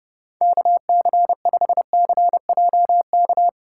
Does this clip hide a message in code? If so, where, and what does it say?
Yes: Morse code from 0.4 to 3.5 s, heard as a keyed tone, about 30 wpm, 700 Hz. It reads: KC5CJK